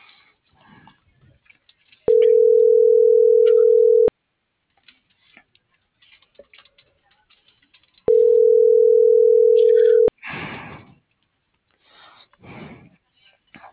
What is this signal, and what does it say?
Ringback tone (North America), 440 + 480 Hz.